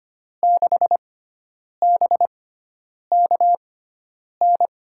Morse code, heard as a keyed tone, about 25 wpm, spaced as for 9 wpm, 700 Hz. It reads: 6BKN
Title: USA Call Signs - 6 Characters - 6x Character Spacing - 25wpm